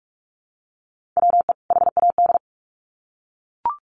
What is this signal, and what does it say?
Morse code, heard as a keyed tone, about 44 wpm, 700 Hz. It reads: WE HAD
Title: Sets of 2 Words 45wpm